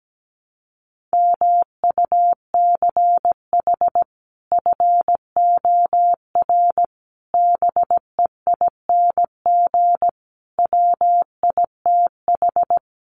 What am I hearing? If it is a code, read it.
Morse code, heard as a keyed tone, about 17 wpm, 700 Hz. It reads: MUCH FOR BEING WITH
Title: Sets of 5 Words 17wpm